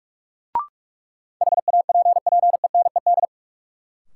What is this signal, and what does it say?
Morse code, heard as a keyed tone, about 46 wpm, 700 Hz. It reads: HAPPENED
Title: Sentences from Top 1000 Words - Review Part 2 - New Word Drill - 45wpm